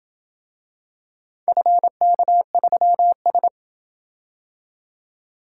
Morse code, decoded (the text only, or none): FK3S